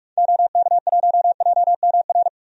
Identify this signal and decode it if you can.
Morse code, heard as a keyed tone, about 45 wpm, 700 Hz. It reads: OK1JMR
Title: DXCC-100 Call Signs - 6 Characters - 45wpm